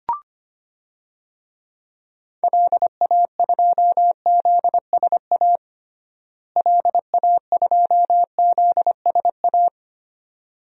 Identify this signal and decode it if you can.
Morse code, heard as a keyed tone, about 25 wpm, 700 Hz. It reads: LA2ZSA LA2ZSA
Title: DXCC-100 Call Signs - 6 Characters - Repeated 2x - 25wpm